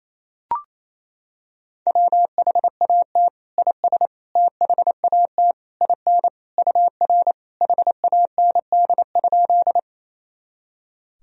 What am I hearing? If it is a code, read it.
Morse code, heard as a keyed tone, about 28 wpm, 700 Hz. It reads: WHAT IS THAT IN UR HAND?
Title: Sentences from Top 200 Words and Abbreviations 28wpm